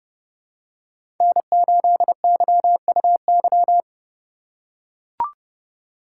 Morse code, heard as a keyed tone, about 30 wpm, 700 Hz. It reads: N8YUY
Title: USA Call Signs - 8x Character Spacing - 30wpm